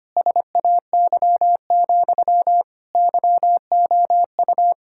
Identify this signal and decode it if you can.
Morse code, heard as a keyed tone, about 25 wpm, 700 Hz. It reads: SAY, YOU
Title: Sentences from Top 1600 Words 25wpm